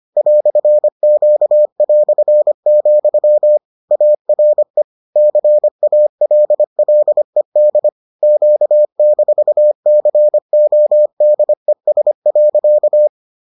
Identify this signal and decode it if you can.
Morse code, heard as a keyed tone, about 25 wpm, 600 Hz. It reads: "Q", ARE CALLED Q-CODES.